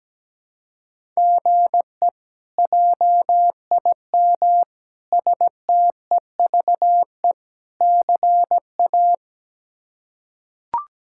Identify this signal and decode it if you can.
Morse code, heard as a keyed tone, about 17 wpm, 700 Hz. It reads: GE JIM STEVE CA
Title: K1USN SST QSO Practice - 17wpm